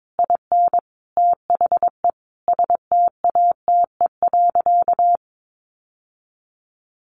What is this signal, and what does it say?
Morse code, heard as a keyed tone, about 22 wpm, 700 Hz. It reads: IN THE STATE.
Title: Sentences from Top 1300 Words 22wpm